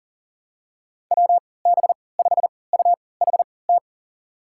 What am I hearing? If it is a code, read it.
Morse code, heard as a keyed tone, about 37 wpm, spaced as for 22 wpm, 700 Hz. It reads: WB5UHT
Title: USA Call Signs - 6 Characters - 3x Character Spacing - 40wpm